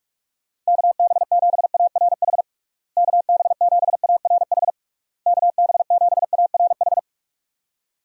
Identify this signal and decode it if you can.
Morse code, heard as a keyed tone, about 46 wpm, 700 Hz. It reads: KB7ARH KB7ARH KB7ARH
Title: USA Call Signs - 6 Characters - Repeated 3x - 45wpm